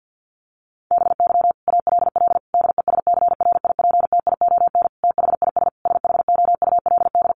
Morse code, heard as a keyed tone, about 49 wpm, 700 Hz. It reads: BY ALL DESCRIPTION THIS SHOULD
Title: Sentences from Top 1300 Words 50wpm